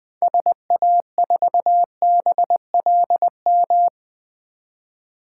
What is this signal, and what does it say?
Morse code, heard as a keyed tone, about 20 wpm, 700 Hz. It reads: SA4BLM
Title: 2023 SOTA Chaser Call Signs - 20wpm